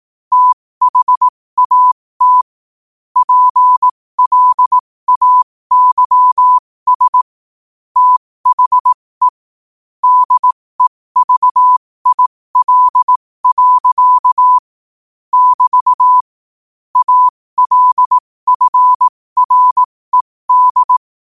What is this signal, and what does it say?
Morse code, heard as a keyed tone, about 18 wpm, spaced as for 15 wpm, 1000 Hz. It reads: THAT PLAYS THE DEVIL. <BT> ALFRED
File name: qotdfiles / 15wpm